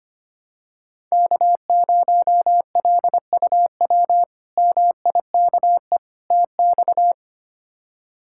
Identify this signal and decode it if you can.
Morse code, heard as a keyed tone, about 25 wpm, 700 Hz. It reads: K0LUW MIKE TX
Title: K1USN SST QSO Practice - 25wpm